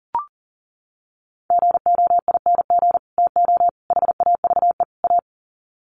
Morse code, heard as a keyed tone, about 40 wpm, 700 Hz. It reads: GOING TO HAVE A
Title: Sets of 4 Words 40wpm